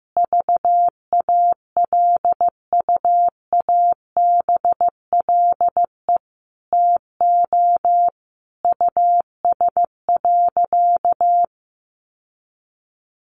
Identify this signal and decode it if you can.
Morse code, heard as a keyed tone, about 15 wpm, 700 Hz. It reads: VALUABLE TO US.
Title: Sentences from Top 1100 Words 15wpm